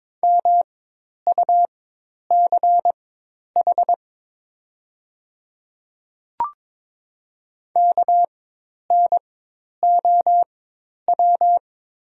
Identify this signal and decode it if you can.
Morse code, heard as a keyed tone, about 22 wpm, spaced as for 10 wpm, 690 Hz. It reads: MUCH KNOW
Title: 4x Character Spacing - Top 200 Words - 22wpm